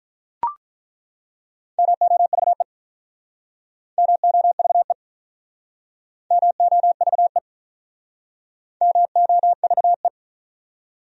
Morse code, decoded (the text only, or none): MOVE MOVE MOVE MOVE